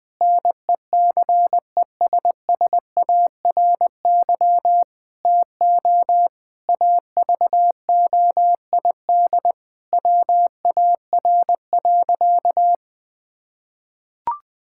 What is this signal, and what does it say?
Morse code, heard as a keyed tone, about 21 wpm, 700 Hz. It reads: NECESSARY TO AVOID WAR.
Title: Sentences from Top 1000 Words - Review Part 6 - 20wpm